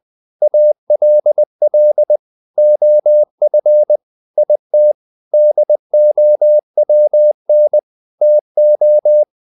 Morse code, heard as a keyed tone, about 20 wpm, 600 Hz. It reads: ALL OF IT DOWN TO
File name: the old man and the sea part 2 20wpm